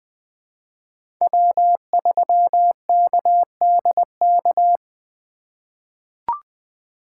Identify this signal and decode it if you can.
Morse code, heard as a keyed tone, about 20 wpm, 700 Hz. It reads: W3KDK
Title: USA Call Signs - 5 Characters - 2x Character Spacing - 20wpm